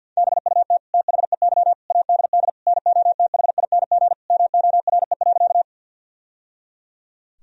Morse code, heard as a keyed tone, about 52 wpm, 700 Hz. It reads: BUT THEY ADD NOTHING MORE.
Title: Sentences from Top 1000 Words - Review Part 1 - 50wpm